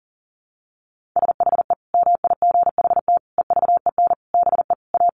Morse code, heard as a keyed tone, about 40 wpm, 700 Hz. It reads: SHE MIGHT EVEN BE A
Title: Sentences from Top 1500 Words 40wpm